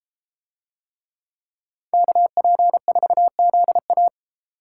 Morse code, heard as a keyed tone, about 32 wpm, 700 Hz. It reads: KP4ZA